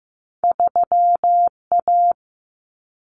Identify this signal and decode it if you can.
Morse code, heard as a keyed tone, about 15 wpm, 700 Hz. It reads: 3A
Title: Common Cut Numbers - 2-Digit 15wpm